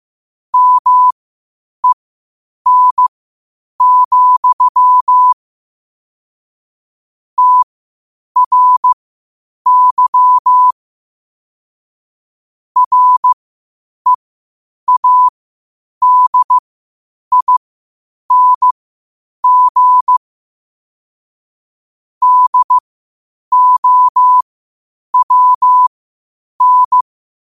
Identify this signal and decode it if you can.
Morse code, heard as a keyed tone, about 15 wpm, spaced as for 8 wpm, 1000 Hz. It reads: MEN, TRY READING DOWN
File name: qotdfiles / 8wpm